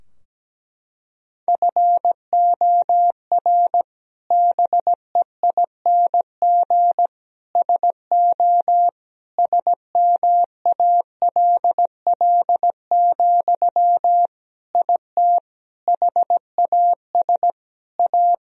Morse code, heard as a keyed tone, about 17 wpm, 700 Hz. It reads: FOR BEING SO SMALL, IT HAS A
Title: Sentences from Top 1000 Words - Review Part 3 - 17wpm